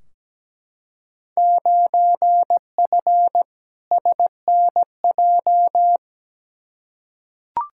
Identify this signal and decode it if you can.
Morse code, heard as a keyed tone, about 17 wpm, 700 Hz. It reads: 9F SNJ
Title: Field Day Exchanges Encore - 17wpm